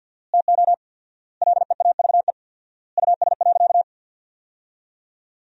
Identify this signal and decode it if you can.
Morse code, heard as a keyed tone, about 50 wpm, 700 Hz. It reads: TO LEAVE US.
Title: 4x Word-Spacing - Sentences from Top 400 Words - 50wpm